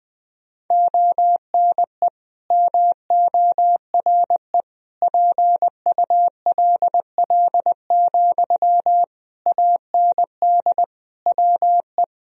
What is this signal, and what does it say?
Morse code, heard as a keyed tone, about 21 wpm, 700 Hz. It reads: ONE MORE PULL, AND WE
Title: Sentences from Top 1000 Words - Review Part 2 - 20wpm